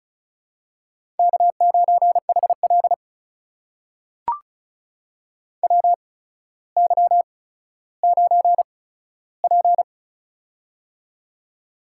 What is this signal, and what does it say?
Morse code, heard as a keyed tone, about 36 wpm, spaced as for 15 wpm, 700 Hz. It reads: K9HL WY9P
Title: USA Call Signs - 4 Characters - 8x Character Spacing - 35wpm